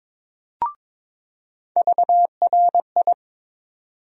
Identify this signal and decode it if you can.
Morse code, heard as a keyed tone, about 21 wpm, 700 Hz. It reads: VRI
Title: USA Call Sign Postfixes - 3 Letters 22wpm